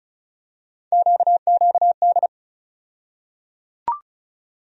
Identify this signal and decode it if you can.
Morse code, heard as a keyed tone, about 36 wpm, 700 Hz. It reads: QQD
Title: USA Call Sign Postfixes - 3 Letters 35wpm